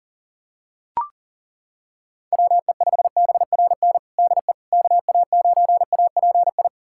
Morse code, heard as a keyed tone, about 40 wpm, 700 Hz. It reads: WE5BRN DE KA9API
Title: USA Callsign DE Callsign - 6 Characters - 40wpm